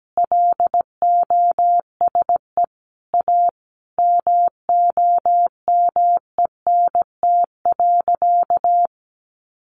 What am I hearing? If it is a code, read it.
Morse code, heard as a keyed tone, about 17 wpm, 700 Hz. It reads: LOSE A MOMENT.
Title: Sentences from Top 1100 Words 17wpm